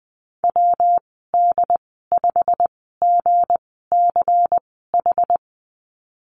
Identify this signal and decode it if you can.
Morse code, heard as a keyed tone, about 19 wpm, spaced as for 14 wpm, 700 Hz. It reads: WD5GCH
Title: 193 Lesson 31g - Letter v - Call Signs 20wpm